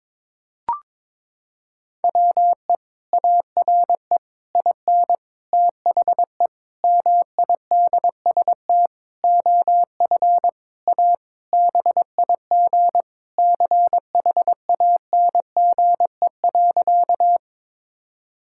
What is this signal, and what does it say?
Morse code, heard as a keyed tone, about 21 wpm, 700 Hz. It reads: WE ARE IN THE MIDST OF A BIG CHANGE.
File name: Sentences with 7-Word Phrases - 22wpm